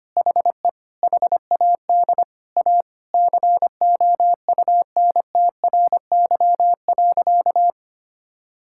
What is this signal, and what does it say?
Morse code, heard as a keyed tone, about 25 wpm, 700 Hz. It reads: HE HAD A COUNTRY.
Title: Sentences from Top 1600 Words 25wpm